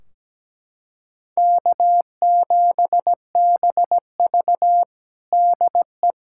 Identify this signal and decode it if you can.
Morse code, heard as a keyed tone, about 17 wpm, 700 Hz. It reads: K7BV DE
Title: Practice QSO - Exchange 4 - Ending - 17wpm